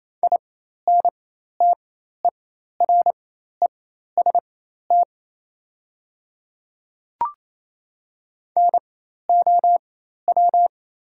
Morse code, heard as a keyed tone, about 28 wpm, spaced as for 13 wpm, 700 Hz. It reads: INTEREST NOW
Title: 4x Character Spacing - Top 500 Words - 28wpm